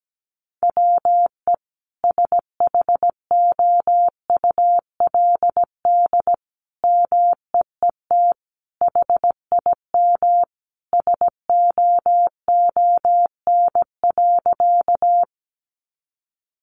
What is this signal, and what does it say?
Morse code, heard as a keyed tone, about 17 wpm, 700 Hz. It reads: WE SHOULD MEET HIM SOON.
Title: Sentences from Top 1100 Words 17wpm